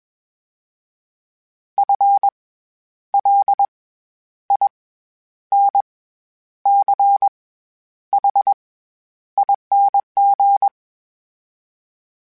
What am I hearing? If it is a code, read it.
Morse code, heard as a keyed tone, about 21 wpm, 800 Hz. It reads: F L I N C H ING
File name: Special INGs-mp3s / Special INGs-mp3s / SlowToFaster ing @20